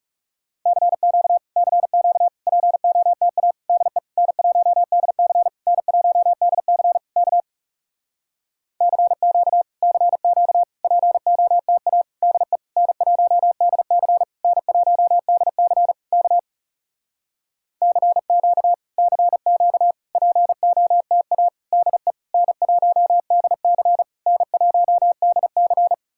Morse code, heard as a keyed tone, about 40 wpm, 700 Hz. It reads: CQ CQ POTA DE N1DC N1DC K CQ CQ POTA DE N1DC N1DC K CQ CQ POTA DE N1DC N1DC